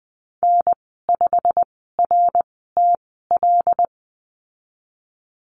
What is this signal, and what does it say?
Morse code, heard as a keyed tone, about 20 wpm, spaced as for 14 wpm, 700 Hz. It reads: N5RTL
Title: 088 Lesson 15g - Number 5 - Call Signs 20wpm